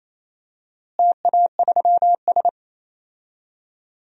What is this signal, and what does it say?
Morse code, heard as a keyed tone, about 27 wpm, 700 Hz. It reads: TA3S
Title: DXCC-100 Call Signs - 4 Characters - Repeated 3x - 28wpm